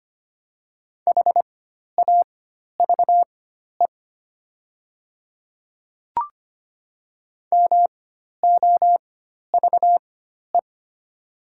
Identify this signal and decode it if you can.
Morse code, heard as a keyed tone, about 25 wpm, spaced as for 12 wpm, 700 Hz. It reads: HAVE MOVE
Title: 4x Character Spacing - Top 200 Words - 25wpm